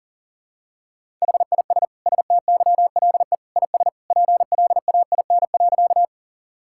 Morse code, heard as a keyed tone, about 40 wpm, 700 Hz. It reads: HIS STYLE IS PLAIN.
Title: Sentences from Top 1000 Words - Review Part 5 - 40wpm